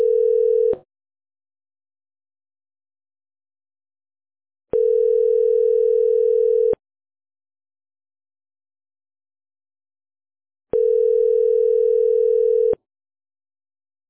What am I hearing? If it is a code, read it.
Ringback tone (North America), 440 + 480 Hz.